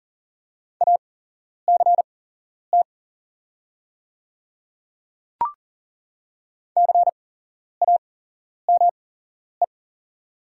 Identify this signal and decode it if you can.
Morse code, heard as a keyed tone, about 40 wpm, spaced as for 11 wpm, 690 Hz. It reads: ACT CAME
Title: 8x Character Spacing - Top 200 Words - 40wpm